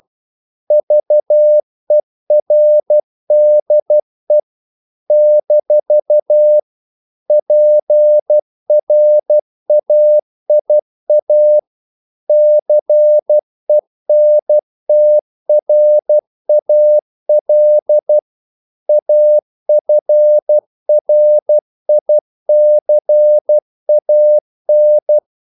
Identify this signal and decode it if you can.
Morse code, heard as a keyed tone, about 12 wpm, 600 Hz. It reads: VERDE - PRAIA CENTRAL AFRICAN
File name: CW / Practice / NationalCapitals12